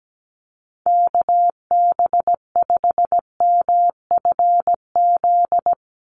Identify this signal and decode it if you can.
Morse code, heard as a keyed tone, about 17 wpm, 700 Hz. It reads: KB5MFZ
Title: USA Call Signs 17wpm